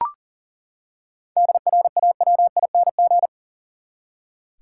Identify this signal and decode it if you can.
Morse code, heard as a keyed tone, about 40 wpm, 700 Hz. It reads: DRAWING
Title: Seven Letter Words - 40wpm